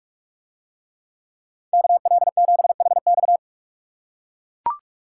Morse code, heard as a keyed tone, about 43 wpm, 700 Hz. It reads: KL7HX